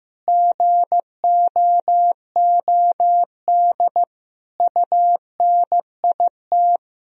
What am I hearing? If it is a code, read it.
Morse code, heard as a keyed tone, about 15 wpm, 700 Hz. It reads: GOOD UNIT
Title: Sentences from Top 1000 Words - Review Part 2 - 15wpm